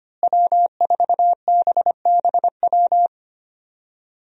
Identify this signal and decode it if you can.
Morse code, heard as a keyed tone, about 25 wpm, 700 Hz. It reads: W4BBW
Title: USA Call Signs - 5 Characters - Repeated 3x - 25wpm